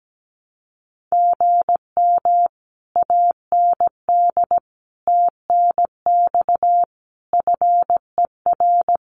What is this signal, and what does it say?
Morse code, heard as a keyed tone, about 17 wpm, 700 Hz. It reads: GM AND TNX FER
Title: Common QSO Phrases Encore 17wpm